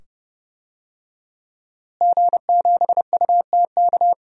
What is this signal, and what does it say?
Morse code, heard as a keyed tone, about 30 wpm, 700 Hz. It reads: G7UTK
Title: DXCC-100 Call Signs - 5 Characters - Repeated 2x - 30wpm